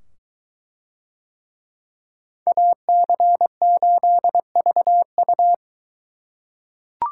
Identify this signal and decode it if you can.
Morse code, heard as a keyed tone, about 21 wpm, 700 Hz. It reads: AC8VU